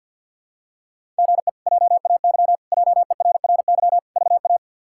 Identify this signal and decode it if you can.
Morse code, heard as a keyed tone, about 50 wpm, 700 Hz. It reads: GE JAY JERRY VA
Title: K1USN SST QSO Practice - 50wpm